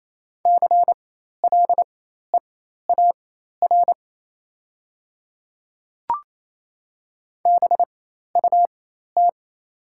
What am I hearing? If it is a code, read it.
Morse code, heard as a keyed tone, about 28 wpm, spaced as for 13 wpm, 700 Hz. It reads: CLEAR BUT
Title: 4x Character Spacing - Top 500 Words - 28wpm